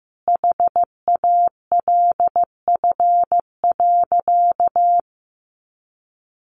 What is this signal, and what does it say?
Morse code, heard as a keyed tone, about 15 wpm, 700 Hz. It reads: HALF.
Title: Sentences from Top 800 Words 15wpm